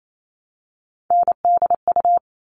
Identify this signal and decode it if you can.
Morse code, heard as a keyed tone, about 28 wpm, 700 Hz. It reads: NDU
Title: All Cut Numbers - 3-Digit 28wpm